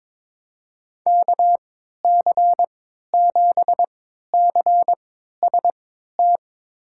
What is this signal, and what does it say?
Morse code, heard as a keyed tone, about 22 wpm, spaced as for 13 wpm, 700 Hz. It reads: KC7CST
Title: USA Call Signs - 6 Characters - 3x Character Spacing - 22wpm